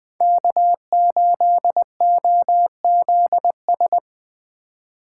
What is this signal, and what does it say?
Morse code, heard as a keyed tone, about 24 wpm, 700 Hz. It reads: K8OZS
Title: Speed Racing - USA Call Signs 20wpm